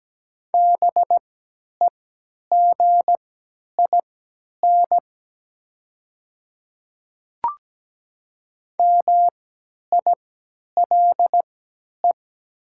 Morse code, heard as a keyed tone, about 17 wpm, spaced as for 10 wpm, 700 Hz. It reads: BEGIN MILE
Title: 3x Character Spacing - Top 500 Words - 17wpm